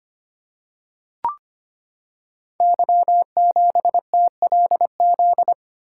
Morse code, heard as a keyed tone, about 25 wpm, 700 Hz. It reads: Y7TLZ
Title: Code Groups - 5 Letter:Numbers - 25wpm